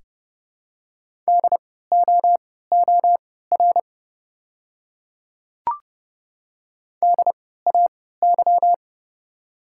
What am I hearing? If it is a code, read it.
Morse code, heard as a keyed tone, about 31 wpm, spaced as for 17 wpm, 690 Hz. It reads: DOOR DAY
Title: 3x Character Spacing - Top 300 Words - 30wpm